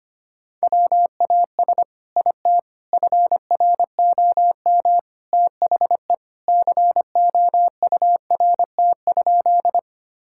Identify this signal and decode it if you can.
Morse code, heard as a keyed tone, about 25 wpm, 700 Hz. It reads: WAS IT FROM THE COURT?
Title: Sentences from Top 1600 Words 25wpm